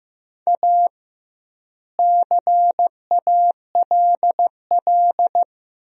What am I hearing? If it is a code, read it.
Morse code, heard as a keyed tone, about 16 wpm, 700 Hz. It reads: A CALL
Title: 2x Word-Spacing - Sentences from Top 300 Words - 15wpm